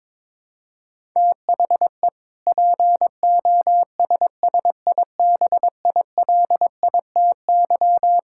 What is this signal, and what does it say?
Morse code, heard as a keyed tone, about 22 wpm, 700 Hz. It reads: THE POSSIBILITY
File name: practice / 22wpm / Sentences with 3-Word Phrases - 22wpm